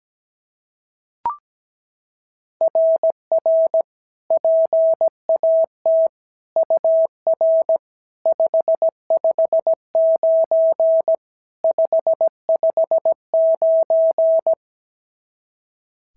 Morse code, heard as a keyed tone, about 16 wpm, 650 Hz. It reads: RR PAT UR 559 559
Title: SOTA QSO Practice 17wpm